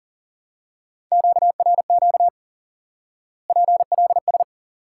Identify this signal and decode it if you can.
Morse code, heard as a keyed tone, about 40 wpm, 700 Hz. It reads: QRQ PLS
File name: IQE Course 3 - Lesson 89 - YAGI - Two Elements - 40wpm